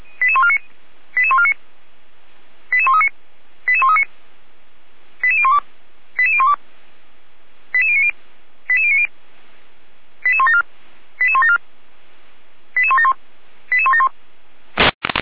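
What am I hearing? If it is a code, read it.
Five-tone selective call (ZVEI), heard as quick runs of five tones, tones of about 70 ms. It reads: 88149 88139 88139 88012 88012 88090 88090 88174 88174 88171 88171
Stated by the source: Five-tone ZVEI selective call, code 88402, received by SDR, from fire brigade Feuerwehr Orlamünde.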